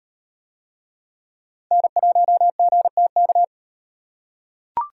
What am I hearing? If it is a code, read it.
Morse code, heard as a keyed tone, about 38 wpm, 700 Hz. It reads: N1GTK